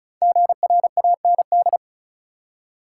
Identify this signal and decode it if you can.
Morse code, heard as a keyed tone, about 34 wpm, 700 Hz. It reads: GRAND
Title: Five Letter Words - 35wpm